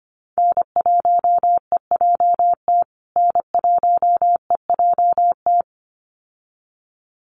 Morse code, heard as a keyed tone, about 25 wpm, 700 Hz. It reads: N1EJT N1EJT
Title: USA Call Signs Repeated 2x 25wpm